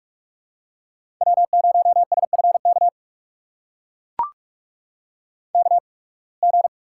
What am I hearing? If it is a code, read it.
Morse code, heard as a keyed tone, about 44 wpm, spaced as for 29 wpm, 700 Hz. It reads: W0SFK KG
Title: USA Call Signs - 5 Characters - 8x Character Spacing - 45wpm